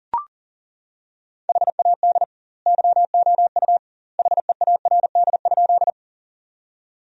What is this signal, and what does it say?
Morse code, heard as a keyed tone, about 38 wpm, 700 Hz. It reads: HAD YOU HEARD?
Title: 2x Word-Spacing - Sentences from Top 400 Words - 40wpm